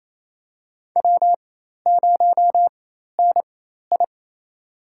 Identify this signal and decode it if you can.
Morse code, heard as a keyed tone, about 28 wpm, spaced as for 13 wpm, 700 Hz. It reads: W0NI
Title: USA Call Signs - 4 Characters - 4x Character Spacing - 28wpm